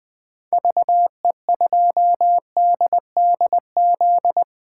Morse code, heard as a keyed tone, about 21 wpm, 700 Hz. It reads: VE2DDZ
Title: 2023 SOTA Chaser Call Signs - 20wpm